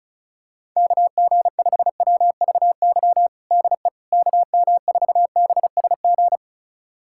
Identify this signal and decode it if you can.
Morse code, heard as a keyed tone, about 34 wpm, 700 Hz. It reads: KG5WVY DE KM4BSG
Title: USA Callsign DE Callsign - 6 Characters - 35wpm